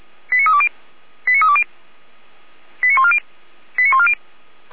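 Five-tone selective call (ZVEI), heard as quick runs of five tones, tones of about 70 ms. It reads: 89320 89320 89140 89140
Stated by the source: Five-tone ZVEI selective call 89223 received by SDR, served by Feuerwehr Orlamünde.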